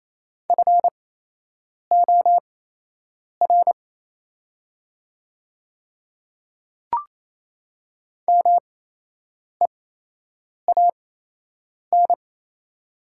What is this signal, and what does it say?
Morse code, heard as a keyed tone, about 28 wpm, spaced as for 8 wpm, 700 Hz. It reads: FOR MEAN
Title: 8x Character Spacing - Top 200 Words - 28wpm